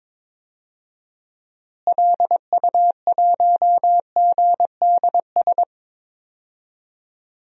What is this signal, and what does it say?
Morse code, heard as a keyed tone, about 22 wpm, 700 Hz. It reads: LU1GDS
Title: DXCC-100 Call Signs - 6 Characters - Repeated 3x - 22wpm